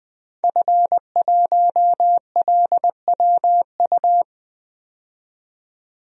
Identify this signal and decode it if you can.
Morse code, heard as a keyed tone, about 21 wpm, 700 Hz. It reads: F1LWU
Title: DXCC-100 Call Signs - 5 Characters - 20wpm